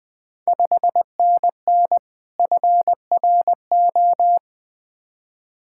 Morse code, heard as a keyed tone, about 20 wpm, 690 Hz. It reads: 5NN FRO
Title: Ontario QSO Party Exchange 20wpm